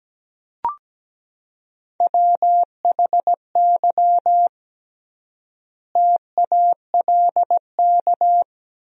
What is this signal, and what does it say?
Morse code, heard as a keyed tone, about 17 wpm, 700 Hz. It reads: WHY TALK
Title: 3x Word-Spacing - Sentences from Top 400 Words - 17wpm